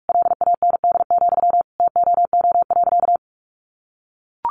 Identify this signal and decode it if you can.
Morse code, heard as a keyed tone, about 44 wpm, 700 Hz. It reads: LAND, TOO.
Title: Sentences from Top 1500 Words 45wpm